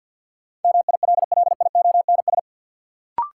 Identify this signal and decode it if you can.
Morse code, heard as a keyed tone, about 53 wpm, 700 Hz. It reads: MILLIONS
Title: Sets of 5 Words - New Word Drill - 50wpm